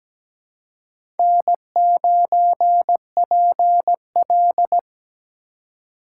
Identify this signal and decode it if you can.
Morse code, heard as a keyed tone, about 17 wpm, 700 Hz. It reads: N9PL
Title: USA Call Signs - 4 Characters - 4x Character Spacing - 17wpm